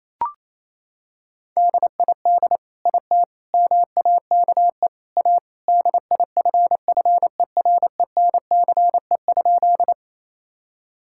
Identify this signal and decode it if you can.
Morse code, heard as a keyed tone, about 28 wpm, 700 Hz. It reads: DID IT MAKE A DIFFERENCE?
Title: Sentences from Top 1000 Words - Review Part 5 - 28wpm